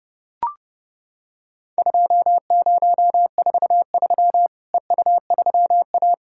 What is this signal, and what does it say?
Morse code, heard as a keyed tone, about 32 wpm, 700 Hz. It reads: 2043 EU3A
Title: Worked All Europe DX Contest - QTC Exchange - 30wpm